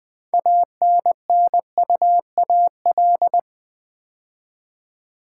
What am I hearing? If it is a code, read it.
Morse code, heard as a keyed tone, about 20 wpm, 700 Hz. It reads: ANNUAL
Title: Build A Word - 3 Letters At A Time - 6-Letter Words - 20wpm